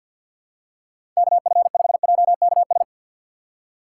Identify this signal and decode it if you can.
Morse code, heard as a keyed tone, about 50 wpm, 700 Hz. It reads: KF5JKS
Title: USA Call Signs - 6 Characters - 6x Character Spacing - 50wpm